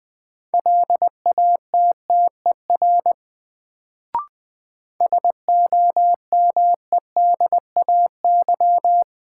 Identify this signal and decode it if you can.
Morse code, heard as a keyed tone, about 20 wpm, 700 Hz. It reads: LATTER SOMEDAY
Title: Sentences from Top 1600 Words - New Word Drill - 20wpm